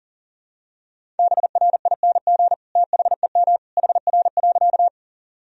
Morse code, heard as a keyed tone, about 37 wpm, 700 Hz. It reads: BRING THEM HR.
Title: Sentences from Top 500 Words and Abbreviations 40wpm